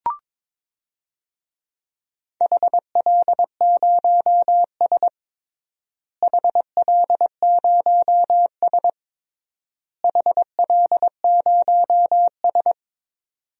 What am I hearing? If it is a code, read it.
Morse code, heard as a keyed tone, about 22 wpm, 700 Hz. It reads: HL0S HL0S HL0S
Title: DXCC-100 Call Signs - 4 Characters - Repeated 3x - 22wpm